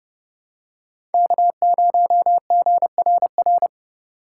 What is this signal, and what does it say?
Morse code, heard as a keyed tone, about 29 wpm, 700 Hz. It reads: K0GRR